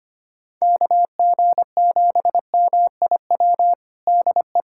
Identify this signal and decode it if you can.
Morse code, heard as a keyed tone, about 25 wpm, 700 Hz. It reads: KG7MIW DE
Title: USA Callsign DE Callsign 25wpm